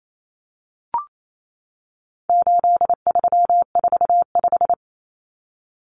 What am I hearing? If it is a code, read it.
Morse code, heard as a keyed tone, about 28 wpm, 700 Hz. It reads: 8345
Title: Common Cut Numbers - 4-Digit 28wpm